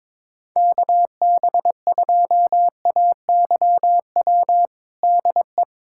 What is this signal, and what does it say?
Morse code, heard as a keyed tone, about 22 wpm, 700 Hz. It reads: KB2AYW DE
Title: USA Callsign DE Callsign - 6 Characters - 22wpm